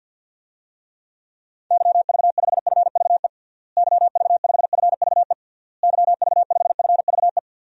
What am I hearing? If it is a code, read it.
Morse code, heard as a keyed tone, about 50 wpm, 700 Hz. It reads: YV5FVE YV5FVE YV5FVE
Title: DXCC-100 Call Signs - 6 Characters - Repeated 3x - 50wpm